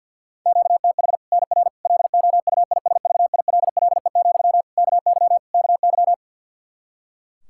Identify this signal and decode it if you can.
Morse code, heard as a keyed tone, about 51 wpm, 700 Hz. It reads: QTH NR LOUISVILLE, KY KY
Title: K1USN SST QSO Practice 50wpm